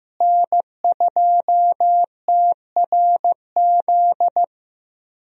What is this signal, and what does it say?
Morse code, heard as a keyed tone, about 15 wpm, 700 Hz. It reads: N2TRZ
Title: USA Call Signs - 5 Characters - Repeated 2x - 15wpm